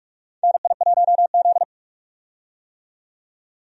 Morse code, heard as a keyed tone, about 46 wpm, 700 Hz. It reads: NI1Z
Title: USA Call Signs - 4 Characters - 45wpm